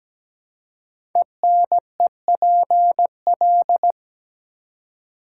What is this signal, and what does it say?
Morse code, heard as a keyed tone, about 17 wpm, 700 Hz. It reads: ENEPL
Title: Code Groups - 5 Letters - 17wpm